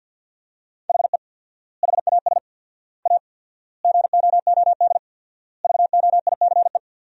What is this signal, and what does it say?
Morse code, heard as a keyed tone, about 50 wpm, 700 Hz. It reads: HE HAS A GOOD VOICE
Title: 4x Word-Spacing - Sentences from Top 500 Words - 50wpm